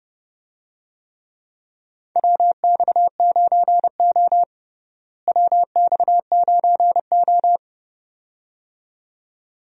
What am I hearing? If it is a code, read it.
Morse code, heard as a keyed tone, about 32 wpm, 700 Hz. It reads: WX9O WX9O
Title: USA Call Signs - 4 Characters - Repeated 2x - 30wpm